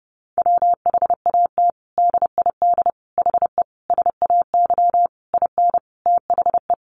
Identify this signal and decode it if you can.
Morse code, heard as a keyed tone, about 30 wpm, 700 Hz. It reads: WHAT DID HE SAY IN THE
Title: Sentences from Top 300 Words 30wpm